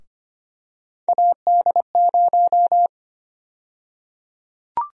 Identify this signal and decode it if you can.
Morse code, heard as a keyed tone, about 25 wpm, 700 Hz. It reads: AD0
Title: USA Call Sign Prefixes - Repeated 2x - 25wpm